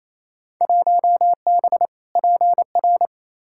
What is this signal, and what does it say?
Morse code, heard as a keyed tone, about 28 wpm, 700 Hz. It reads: 1B PR
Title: Field Day Exchanges Encore - 28wpm